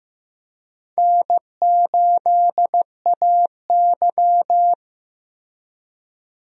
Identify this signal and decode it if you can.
Morse code, heard as a keyed tone, about 15 wpm, 700 Hz. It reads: N8AY
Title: USA Call Signs - 4 Characters - Repeated 3x - 15wpm